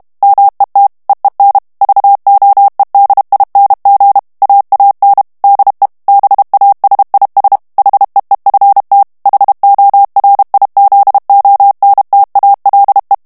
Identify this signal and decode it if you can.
Morse code, heard as a keyed tone, about 32 wpm, 800 Hz. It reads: MET EEN VOEDING AAN DE BASIS HEEFT HORIZONTALE
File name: HOLLANDAIS texte clair 1 vitesse 30 mots minute